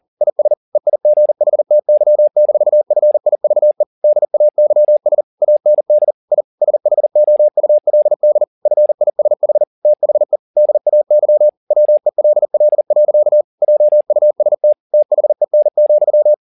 Morse code, heard as a keyed tone, about 40 wpm, 600 Hz. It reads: IS EIGHTY-FIVE DAYS AND I SHOULD FISH THE DAY WELL. JUST THEN,